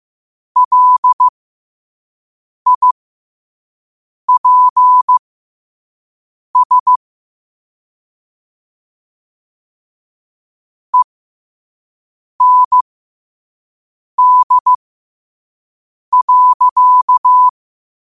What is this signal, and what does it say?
Morse code, heard as a keyed tone, about 15 wpm, spaced as for 5 wpm, 1000 Hz. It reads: LIPS END.
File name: qotdfiles / 5wpm